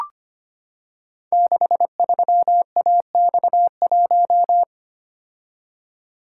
Morse code, heard as a keyed tone, about 25 wpm, 700 Hz. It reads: 63AX1
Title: Code Groups - 5 Letter:Numbers - 25wpm